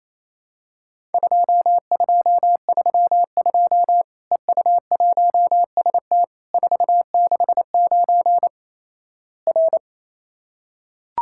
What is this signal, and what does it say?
Morse code, heard as a keyed tone, about 28 wpm, 700 Hz. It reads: 2232 EU1ST 469 R
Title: Worked All Europe DX Contest - QTC Exchange - 28wpm